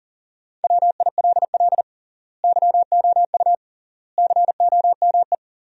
Morse code, heard as a keyed tone, about 40 wpm, 700 Hz. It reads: WILL YOU COME
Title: 3x Word-Spacing - Sentences from Top 100 Words - 40wpm